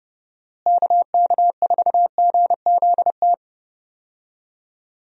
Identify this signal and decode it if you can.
Morse code, heard as a keyed tone, about 32 wpm, 700 Hz. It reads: KK4GZT
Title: USA Call Signs - 6 Characters - Repeated 3x - 30wpm